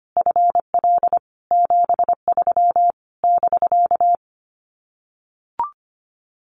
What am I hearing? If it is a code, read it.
Morse code, heard as a keyed tone, about 25 wpm, 700 Hz. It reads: FL 73 <BK>
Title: Common QSO Phrases Encore 25wpm